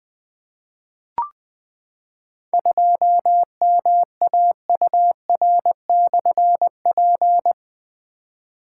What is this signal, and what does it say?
Morse code, heard as a keyed tone, about 20 wpm, 700 Hz. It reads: 2MAUR/P
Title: Difficult Call Signs 20wpm